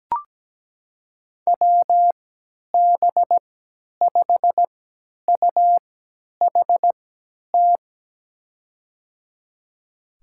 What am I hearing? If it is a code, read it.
Morse code, heard as a keyed tone, about 17 wpm, spaced as for 10 wpm, 700 Hz. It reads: WB5UHT
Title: USA Call Signs - 3x Character Spacing - 17wpm